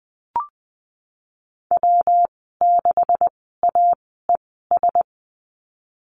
Morse code, heard as a keyed tone, about 20 wpm, spaced as for 15 wpm, 700 Hz. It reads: W6AES
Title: 128 Lesson 21g - Number 6 - Call Signs 20wpm